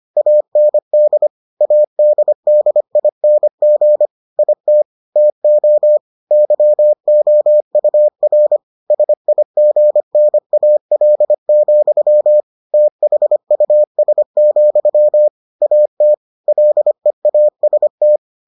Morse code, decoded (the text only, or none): AND ADDING IT TO YOUR SIGNAL, THUS, AT LEAST